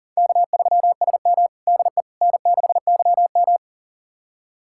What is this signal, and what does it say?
Morse code, heard as a keyed tone, about 37 wpm, 700 Hz. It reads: K3SM DE N6YM
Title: USA Callsign DE Callsign - 4 Characters - 40wpm